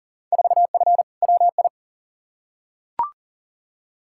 Morse code, heard as a keyed tone, about 37 wpm, 700 Hz. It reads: 4F WI
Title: Field Day Exchanges Encore - 40wpm